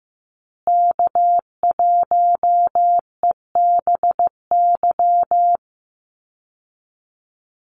Morse code, heard as a keyed tone, about 15 wpm, 700 Hz. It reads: K1EBY
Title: Call Signs Repeated 2x - 15wpm